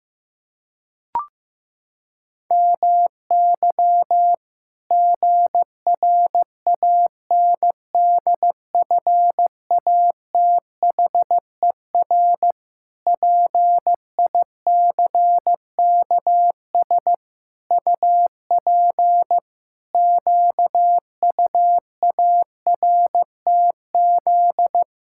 Morse code, heard as a keyed tone, about 16 wpm, 700 Hz. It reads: MY GRANDFATHER PICKS UP QUARTZ
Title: Pangram Challenge v2 - 15wpm